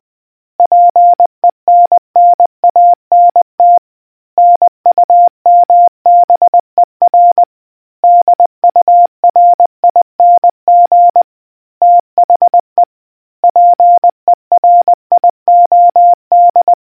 Morse code, heard as a keyed tone, about 20 wpm, 700 Hz. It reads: PENNANT NUMBER DURING THE PERIOD